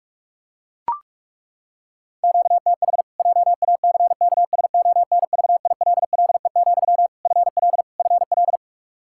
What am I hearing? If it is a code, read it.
Morse code, heard as a keyed tone, about 42 wpm, 700 Hz. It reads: QTH JACKSONVILLE, FL FL
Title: Practice QSO - Exchange 1 - Beginning - 45wpm